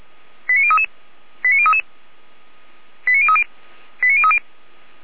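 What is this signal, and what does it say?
Five-tone selective call (ZVEI), heard as quick runs of five tones, tones of about 70 ms. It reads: 89033 89033 89030 89030
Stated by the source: ZVEI five-tone selective call received via SDR, from the Feuerwehr Orlamünde fire brigade audio server.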